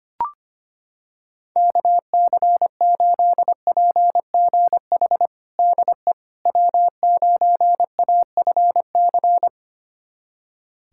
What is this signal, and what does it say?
Morse code, heard as a keyed tone, about 25 wpm, 700 Hz. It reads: KC8PGH DE W9AFC
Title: USA Callsign DE Callsign 25wpm